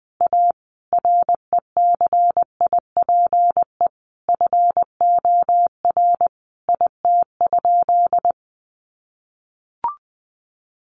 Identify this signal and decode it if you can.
Morse code, heard as a keyed tone, about 20 wpm, 700 Hz. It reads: A RECIPE FOR IT?
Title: Sentences from Top 1200 Words 20wpm